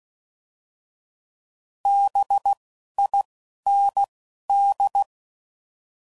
Morse code, heard as a keyed tone, about 16 wpm, 800 Hz. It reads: B I N D